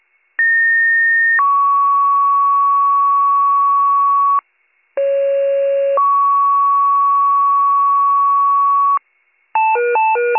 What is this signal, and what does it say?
Two-tone page, heard as a one-second tone, then a three-second tone, about 1850 then 1120 Hz, about 550 then 1120 Hz. It is heard from 0.4 to 9.0 s.